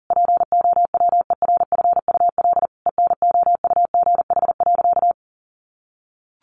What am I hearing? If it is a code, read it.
Morse code, heard as a keyed tone, about 40 wpm, 700 Hz. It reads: POWERFUL ENOUGH.
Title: Sentences from Top 1300 Words 40wpm